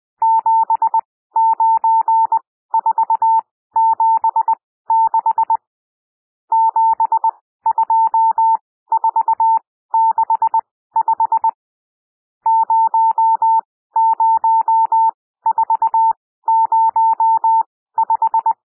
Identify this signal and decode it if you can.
Morse code, heard as a keyed tone, about 20 wpm, spaced as for 14 wpm, 900 Hz. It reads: 79476 72465 00405
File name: cwmp3 / N1 SOUND NUMBERS 20 wpm